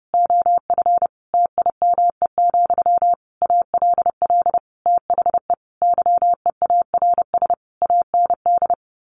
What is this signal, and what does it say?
Morse code, heard as a keyed tone, about 30 wpm, 700 Hz. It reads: OF TIME, ALL THE YEARS AND
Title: A Sound Of Thunder 30wpm